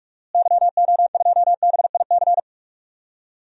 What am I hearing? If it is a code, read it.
Morse code, heard as a keyed tone, about 43 wpm, 700 Hz. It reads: YO2BIC
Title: DXCC-100 Call Signs - 6 Characters - Repeated 3x - 45wpm